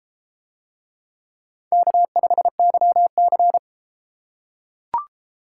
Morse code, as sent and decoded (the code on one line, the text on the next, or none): -.- ..... -.-- -.-.
K5YC